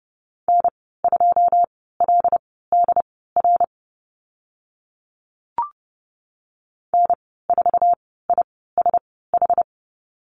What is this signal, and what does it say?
Morse code, heard as a keyed tone, about 30 wpm, spaced as for 17 wpm, 700 Hz. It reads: N2LDR N4ISH
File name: Lesson 15g - Number 5 - Call Signs 30wpm